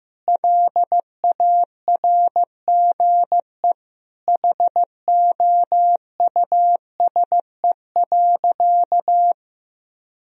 Morse code, heard as a keyed tone, about 15 wpm, 700 Hz. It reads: LARGE HOUSE.
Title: Sentences from Top 1000 Words - Review Part 1 - 15wpm